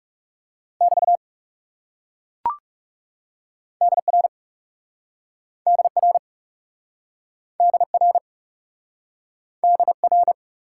Morse code, heard as a keyed tone, about 37 wpm, 700 Hz. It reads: <BT> DR DR DR DR
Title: Speed-Racing Top 90 QSO Elements 30wpm